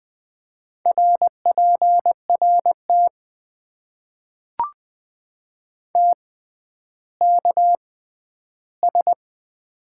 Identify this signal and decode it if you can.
Morse code, heard as a keyed tone, about 20 wpm, spaced as for 11 wpm, 700 Hz. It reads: RPRT TKS
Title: Top 90 QSO Elements - 6x Character Spacing - 20wpm